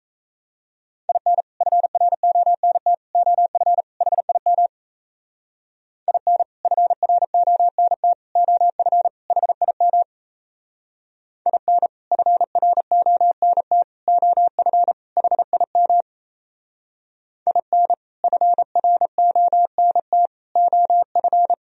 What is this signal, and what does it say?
Morse code, heard as a keyed tone, about 34 wpm, 700 Hz. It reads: IN FRONT OF HIM IN FRONT OF HIM IN FRONT OF HIM IN FRONT OF
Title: Speed-Racing - Sets of 4 Words 28wpm